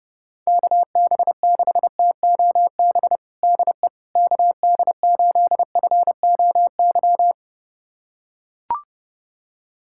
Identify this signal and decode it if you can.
Morse code, heard as a keyed tone, about 30 wpm, 700 Hz. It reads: KB6TOB DE KD8FOY
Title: USA Callsign DE Callsign 30wpm